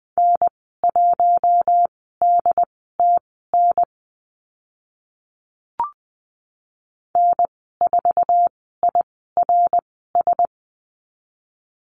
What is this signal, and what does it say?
Morse code, heard as a keyed tone, about 20 wpm, spaced as for 15 wpm, 700 Hz. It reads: N1DTN N4IRS
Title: 082 Lesson 14g - Number 2 - Call Signs 20wpm